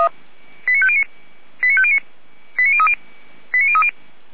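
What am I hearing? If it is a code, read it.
Five-tone selective call (ZVEI), heard as quick runs of five tones, tones of about 70 ms. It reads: 89509 89509 89030 89030
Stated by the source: ZVEI five-tone selective call 89001, received by SDR, from the Feuerwehr Orlamünde audio server.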